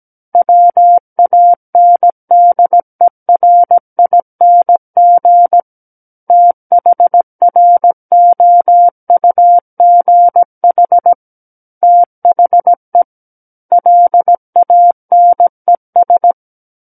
Morse code, heard as a keyed tone, about 17 wpm, 700 Hz. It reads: WANDERING THROUGH THE LANES
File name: Book One / 08-Chapter 8 - 17 WPM